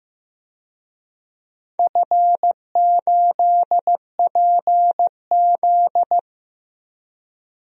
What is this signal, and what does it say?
Morse code, heard as a keyed tone, about 15 wpm, 700 Hz. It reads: F8PZ
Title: DXCC-100 Call Signs - 4 Characters - 15wpm